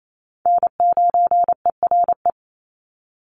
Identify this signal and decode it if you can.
Morse code, heard as a keyed tone, about 28 wpm, 700 Hz. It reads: N9ERE
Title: USA Call Signs 28wpm